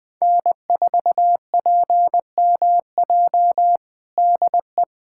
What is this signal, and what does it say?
Morse code, heard as a keyed tone, about 21 wpm, 700 Hz. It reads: N4PMJ DE
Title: USA Callsign DE Callsign - 5 Characters - 20wpm